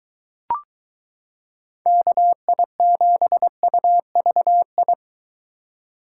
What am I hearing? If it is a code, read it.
Morse code, heard as a keyed tone, about 22 wpm, 700 Hz. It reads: KI7UVI